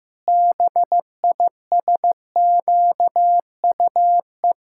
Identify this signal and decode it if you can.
Morse code, heard as a keyed tone, about 15 wpm, 700 Hz. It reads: BISQUE
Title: Pangram Challenge v2 - 15wpm